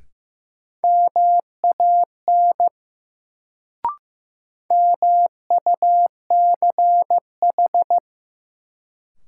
Morse code, heard as a keyed tone, about 15 wpm, 700 Hz. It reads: MAN MUCH
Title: Sets of 2 Words - New Word Drill - 15wpm